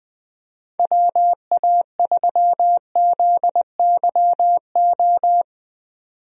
Morse code, heard as a keyed tone, about 20 wpm, 700 Hz. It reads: WA3ZYO
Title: USA Call Signs - 6 Characters - 20wpm